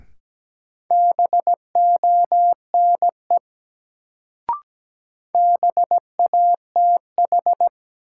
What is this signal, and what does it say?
Morse code, heard as a keyed tone, about 17 wpm, 700 Hz. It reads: BONE BATH
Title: Sentences from Top 1600 Words - New Word Drill Alt - 17wpm